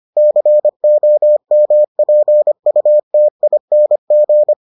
Morse code, heard as a keyed tone, about 25 wpm, 600 Hz. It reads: COMPUTING